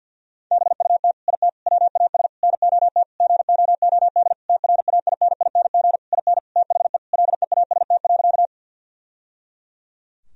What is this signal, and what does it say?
Morse code, heard as a keyed tone, about 50 wpm, 700 Hz. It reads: BUT IT WAS NOT GOOD TRAINING IN THE LEAST.
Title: Sentences from Top 1000 Words - Review Part 4 - 50wpm